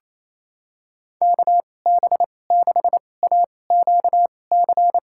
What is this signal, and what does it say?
Morse code, heard as a keyed tone, about 27 wpm, spaced as for 20 wpm, 700 Hz. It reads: KB6AQC
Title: USA Call Signs - 6 Characters - 2x Character Spacing - 28wpm